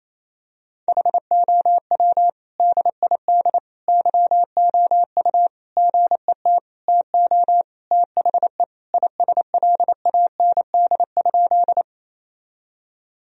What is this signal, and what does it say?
Morse code, heard as a keyed tone, about 27 wpm, 700 Hz. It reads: HOW DID YOU GET TO THE ISLAND?
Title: Sentences from Top 1000 Words - Review Part 3 - 28wpm